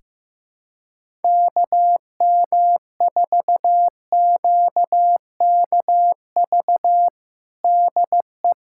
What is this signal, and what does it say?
Morse code, heard as a keyed tone, about 15 wpm, 700 Hz. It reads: KM4QKV DE
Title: USA Callsign DE Callsign - 6 Characters - 15wpm